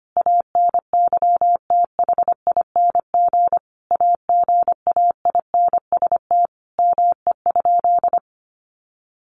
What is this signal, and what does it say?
Morse code, heard as a keyed tone, about 25 wpm, 700 Hz. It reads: ANYTHING AGAINST ME?
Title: Sentences from Top 1000 Words 25wpm